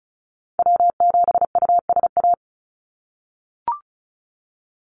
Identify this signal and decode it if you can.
Morse code, heard as a keyed tone, about 34 wpm, 700 Hz. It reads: W7USA
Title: NA SOTA Activator Callsigns 35wpm